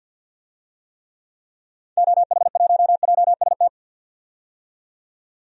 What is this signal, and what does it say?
Morse code, heard as a keyed tone, about 49 wpm, 700 Hz. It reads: OH1JST
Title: DXCC-100 Call Signs - 6 Characters - 50wpm